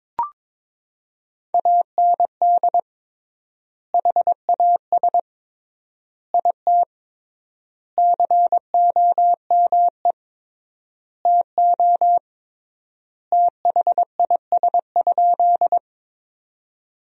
Morse code, heard as a keyed tone, about 22 wpm, 700 Hz. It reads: AND HAS IT COME TO THIS?
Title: 3x Word-Spacing - Sentences from Top 100 Words - 22wpm